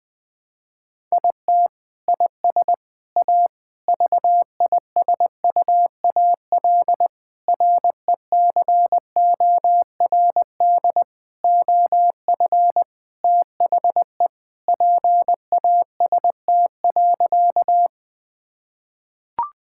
Morse code, heard as a keyed tone, about 20 wpm, 700 Hz. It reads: IT IS A VISUAL RECORD OF THE PAST.
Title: Sentences from Top 1000 Words - Review Part 6 - 20wpm